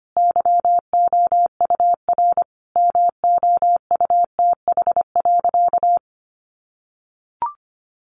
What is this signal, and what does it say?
Morse code, heard as a keyed tone, about 25 wpm, 700 Hz. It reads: YOUR MOUTH.
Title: Sentences from Top 1500 Words 25wpm